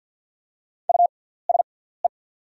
Morse code, heard as a keyed tone, about 48 wpm, spaced as for 17 wpm, 690 Hz. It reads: USE
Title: 6x Character Spacing - Top 500 Words - 50wpm